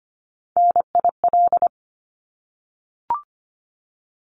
Morse code, heard as a keyed tone, about 25 wpm, 700 Hz. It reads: NIL
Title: Abbreviations 25wpm